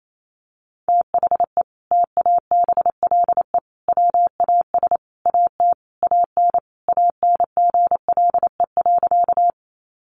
Morse code, heard as a keyed tone, about 28 wpm, 700 Hz. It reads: THE TABLE WAS AT AN ANGLE.
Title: Sentences from Top 1500 Words 28wpm